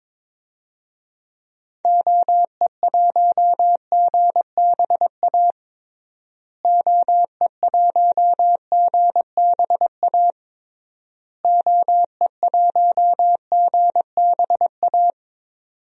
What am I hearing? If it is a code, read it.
Morse code, heard as a keyed tone, about 22 wpm, 700 Hz. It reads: OE1GBA OE1GBA OE1GBA
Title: DXCC-100 Call Signs - 6 Characters - Repeated 3x - 22wpm